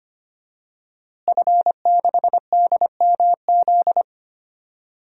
Morse code, heard as a keyed tone, about 25 wpm, 700 Hz. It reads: F6DMZ
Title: DXCC-100 Call Signs - 5 Characters - Repeated 2x - 25wpm